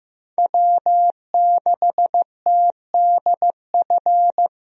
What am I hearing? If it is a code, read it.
Morse code, heard as a keyed tone, about 15 wpm, 700 Hz. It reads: W6TDF
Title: USA Call Signs - 5 Characters - Repeated 3x - 15wpm